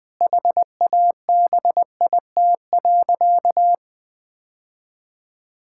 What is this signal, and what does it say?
Morse code, heard as a keyed tone, about 20 wpm, 700 Hz. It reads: HABIT.
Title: Difficult Sentences with Hints - 20wpm